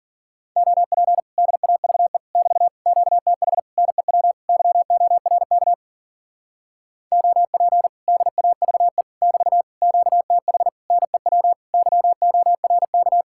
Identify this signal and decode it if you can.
Morse code, heard as a keyed tone, about 52 wpm, 700 Hz. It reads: OP DAVE <BT> QTH NEW YORK OP DAVE <BT> QTH NEW YORK